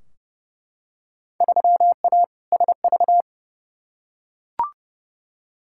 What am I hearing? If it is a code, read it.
Morse code, heard as a keyed tone, about 30 wpm, 700 Hz. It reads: 3A SV
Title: Field Day Exchanges - 30wpm